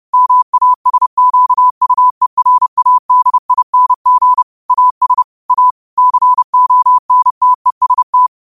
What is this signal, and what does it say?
Morse code, heard as a keyed tone, about 30 wpm, 1000 Hz. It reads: MASQUERADING AS A CONTEST
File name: qotdfiles / 30wpm